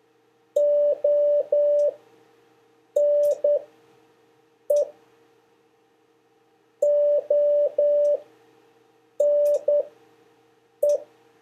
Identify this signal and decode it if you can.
Morse code, heard as a keyed tone, about 10 wpm, spaced as for 6 wpm, 580 Hz. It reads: ONE ONE
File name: 4of5-words-3